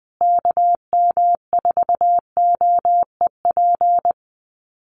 Morse code, heard as a keyed tone, about 20 wpm, 700 Hz. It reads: KM4OEP
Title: USA Call Signs Repeated 3x 20wpm